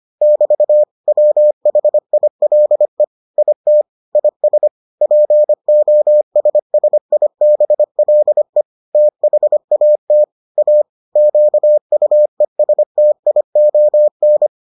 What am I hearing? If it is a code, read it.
Morse code, heard as a keyed tone, about 25 wpm, 600 Hz. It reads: <BT> WHILE IT IS POSSIBLE THAT A QUESTION